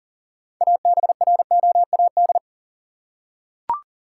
Morse code, heard as a keyed tone, about 40 wpm, 700 Hz. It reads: ABROAD
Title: Six Letter Words - 40wpm